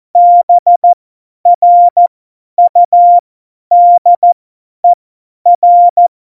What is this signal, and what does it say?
Morse code, heard as a keyed tone, about 14 wpm, spaced as for 10 wpm, 700 Hz. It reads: BRUDER
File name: Morsekurs / Witze / Witz 03. Oktober 17, Tempo 50 F70